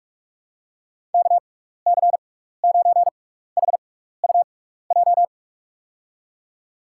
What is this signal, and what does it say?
Morse code, heard as a keyed tone, about 48 wpm, spaced as for 16 wpm, 700 Hz. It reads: KC9HUJ
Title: USA Call Signs - 6 Characters - 6x Character Spacing - 45wpm